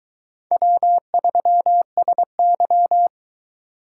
Morse code, heard as a keyed tone, about 22 wpm, 700 Hz. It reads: W3SY